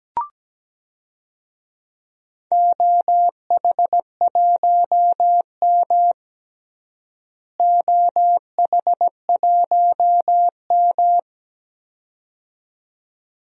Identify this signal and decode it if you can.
Morse code, heard as a keyed tone, about 17 wpm, 700 Hz. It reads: OH1M OH1M
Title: DXCC-100 Call Signs - 4 Characters - Repeated 2x - 17wpm